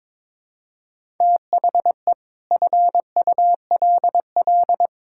Morse code, heard as a keyed tone, about 22 wpm, 700 Hz. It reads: THE FULL
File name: practice / 22wpm / Sentences with 8-Word Phrases - 22wpm